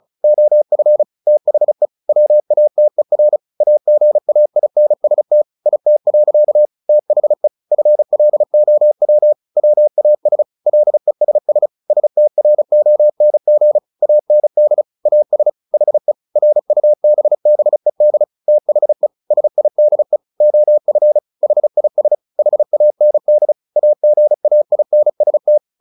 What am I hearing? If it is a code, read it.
Morse code, heard as a keyed tone, about 34 wpm, 600 Hz. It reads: OF THE WATER AGAINST IT. THE FLOW WAS LESS STRONG AND AS HE RUBBED THE SIDE OF HIS HAND AGAINST
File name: the old man and the sea 2 35wpm